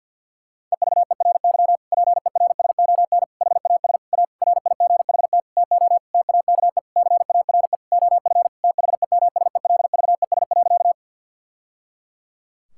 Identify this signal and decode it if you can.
Morse code, heard as a keyed tone, about 52 wpm, 700 Hz. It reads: EVERY PERSON HAS A RIGHT TO TAKE CARE OF THEMSELVES.
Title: Sentences from Top 1000 Words - Review Part 5 - 50wpm